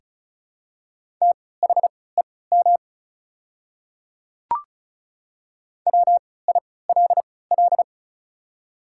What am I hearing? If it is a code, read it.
Morse code, heard as a keyed tone, about 36 wpm, spaced as for 20 wpm, 700 Hz. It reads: THEM WILL
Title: 3x Character Spacing - Top 100 Words - 35wpm